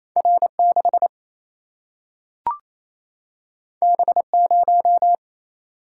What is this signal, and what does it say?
Morse code, heard as a keyed tone, about 28 wpm, 700 Hz. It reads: R6 B0
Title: Single-Letter Single-Number 28wpm